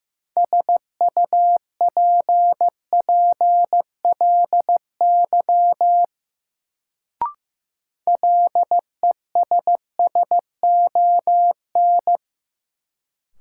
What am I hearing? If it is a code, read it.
Morse code, heard as a keyed tone, about 15 wpm, 700 Hz. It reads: SUPPLY LESSON
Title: Sentences from Top 1200 Words - New Word Drill Alt - 15wpm